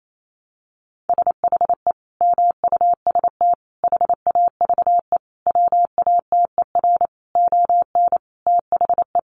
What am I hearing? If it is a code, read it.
Morse code, heard as a keyed tone, about 28 wpm, 700 Hz. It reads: SHE MUST HAVE WATER ON THE
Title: Sentences from Top 1100 Words 28wpm